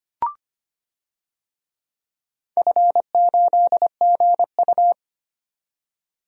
Morse code, heard as a keyed tone, about 25 wpm, 700 Hz. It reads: F8GU
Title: DXCC-100 Call Signs - 4 Characters - 25wpm